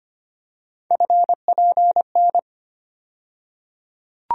Morse code, heard as a keyed tone, about 25 wpm, 700 Hz. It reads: FPN
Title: USA Call Sign Postfixes - 3 Letters 25wpm